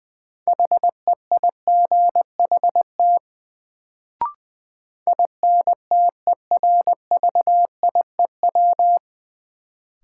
Morse code, heard as a keyed tone, about 20 wpm, 700 Hz. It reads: HEIGHT INTERVIEW
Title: Sentences from Top 1600 Words - New Word Drill - 20wpm